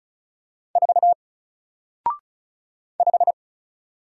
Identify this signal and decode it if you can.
Morse code, heard as a keyed tone, about 36 wpm, 700 Hz. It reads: 4 5
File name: Single-Letter-Number-35wpm